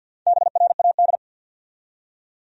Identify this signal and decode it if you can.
Morse code, heard as a keyed tone, about 52 wpm, 700 Hz. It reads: BRAD
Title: USA Common Names - Max 4 Letters - 50wpm